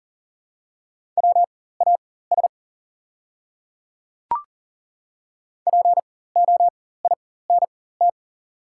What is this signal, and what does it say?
Morse code, heard as a keyed tone, about 38 wpm, spaced as for 18 wpm, 700 Hz. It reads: WAS POINT
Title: 4x Character Spacing - Top 200 Words - 40wpm